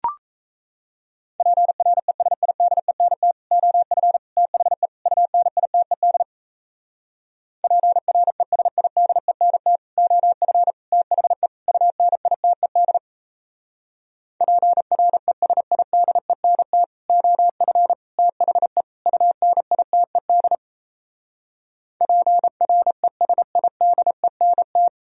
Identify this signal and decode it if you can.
Morse code, heard as a keyed tone, about 37 wpm, 700 Hz. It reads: PRESIDENT OF THE UNITED PRESIDENT OF THE UNITED PRESIDENT OF THE UNITED PRESIDENT